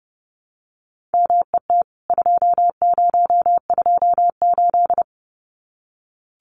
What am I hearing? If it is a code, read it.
Morse code, heard as a keyed tone, about 30 wpm, 700 Hz. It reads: MET 2028
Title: CWT Exchange 30wpm